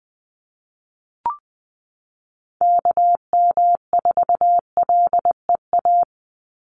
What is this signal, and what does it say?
Morse code, heard as a keyed tone, about 20 wpm, 700 Hz. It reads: KM4LEA
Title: USA Call Signs 20wpm